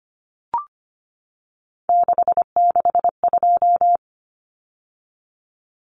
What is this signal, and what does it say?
Morse code, heard as a keyed tone, about 25 wpm, 700 Hz. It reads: 662
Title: Common Cut Numbers - 3-Digit 25wpm